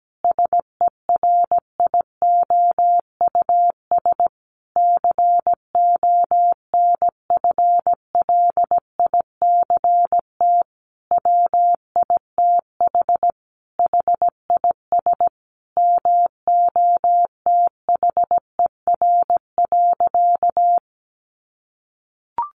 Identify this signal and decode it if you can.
Morse code, heard as a keyed tone, about 17 wpm, 700 Hz. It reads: SERIOUS CONFLICT WITH HIS MOTHER.
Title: Sentences from Top 1300 Words 17wpm